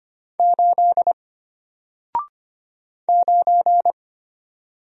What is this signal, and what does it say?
Morse code, heard as a keyed tone, about 25 wpm, 700 Hz. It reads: 8 9
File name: Single-Letter-Number-25wpm